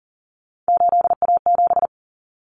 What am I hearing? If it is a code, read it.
Morse code, heard as a keyed tone, about 40 wpm, 700 Hz. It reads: 8A7
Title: Common Cut Numbers - 3-Digit 40wpm